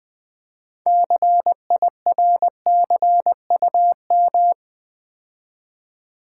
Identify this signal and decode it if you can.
Morse code, heard as a keyed tone, about 20 wpm, 700 Hz. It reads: CIRCUM
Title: Build A Word - 6 Letters At A Time - 12-Letter Words - 20wpm